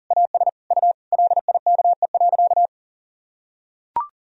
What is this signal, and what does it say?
Morse code, heard as a keyed tone, about 40 wpm, 700 Hz. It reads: AS U LIKE.
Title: Sentences from Top 500 Words and Abbreviations 40wpm